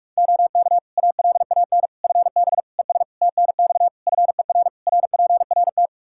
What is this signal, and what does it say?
Morse code, heard as a keyed tone, about 46 wpm, 700 Hz. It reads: OK ALAN FB ES TNX FER RPRT
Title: Practice QSO - Exchange 2 - Info - 45wpm